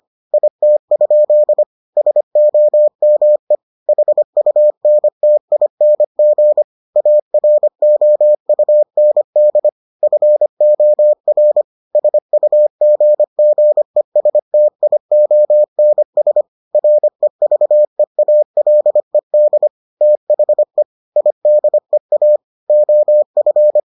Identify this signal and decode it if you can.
Morse code, heard as a keyed tone, about 25 wpm, 600 Hz. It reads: IT? SOME HUNTING AROUND FOR SUGGESTIONS REVEALED THE IDEA OF